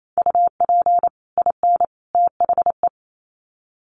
Morse code, heard as a keyed tone, about 28 wpm, 700 Hz. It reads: UP IN THE
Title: Sets of 3 Words 28wpm